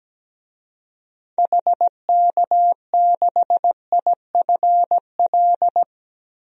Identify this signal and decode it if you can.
Morse code, heard as a keyed tone, about 17 wpm, 700 Hz. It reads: HK6IFL
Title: DXCC-100 Call Signs - 6 Characters - Repeated 2x - 17wpm